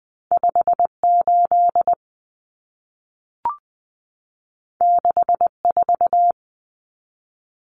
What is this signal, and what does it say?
Morse code, heard as a keyed tone, about 20 wpm, 700 Hz. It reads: 58 64
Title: Common Cut Numbers - 2-Digit 20wpm